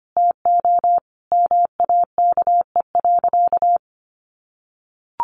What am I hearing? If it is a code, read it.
Morse code, heard as a keyed tone, about 25 wpm, 700 Hz. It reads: TO MAKE.
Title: Sentences from Top 1000 Words 25wpm